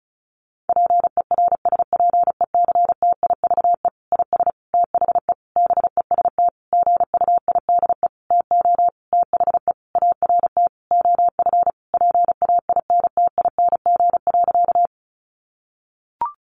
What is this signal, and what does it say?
Morse code, heard as a keyed tone, about 34 wpm, 700 Hz. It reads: PERSPECTIVE IS THE BEST GUIDE TO THE ART OF PAINTING.
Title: Sentences from Top 1200 Words 35wpm